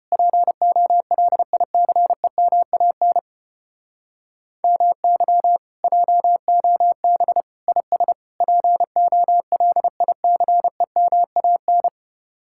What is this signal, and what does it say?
Morse code, heard as a keyed tone, about 33 wpm, 700 Hz. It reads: POLICEMAN MY JOB IS POLICEMAN